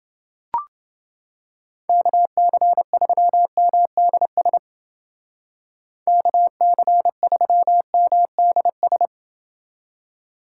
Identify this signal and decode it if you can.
Morse code, heard as a keyed tone, about 29 wpm, 700 Hz. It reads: KC3MDS KC3MDS